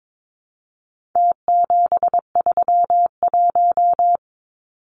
Morse code, heard as a keyed tone, about 22 wpm, 700 Hz. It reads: T731
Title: Numbers - Contest Style - 4-Digit 22wpm